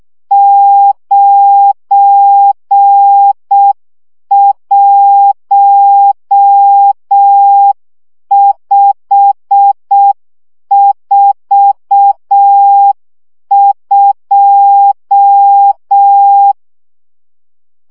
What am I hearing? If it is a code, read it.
Morse code, heard as a keyed tone, about 6 wpm, 800 Hz. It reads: 91542
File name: morze vizsga 2011-1 5